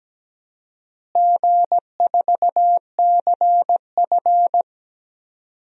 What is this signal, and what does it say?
Morse code, heard as a keyed tone, about 17 wpm, 700 Hz. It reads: G4CF
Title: DXCC-100 Call Signs - 4 Characters - Repeated 3x - 17wpm